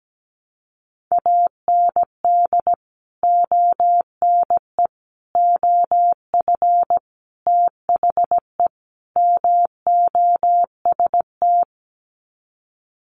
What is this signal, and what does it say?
Morse code, heard as a keyed tone, about 17 wpm, 700 Hz. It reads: AND ONE OF THE MOST
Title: Sets of 5 Words Encore 17wpm